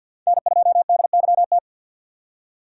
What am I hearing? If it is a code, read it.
Morse code, heard as a keyed tone, about 52 wpm, 700 Hz. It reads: N2DYT
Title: K1USN SST QSO Practice - 50wpm